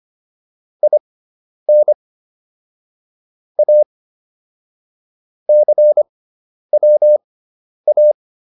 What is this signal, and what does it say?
Morse code, heard as a keyed tone, about 25 wpm, spaced as for 10 wpm, 600 Hz. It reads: IN A CWA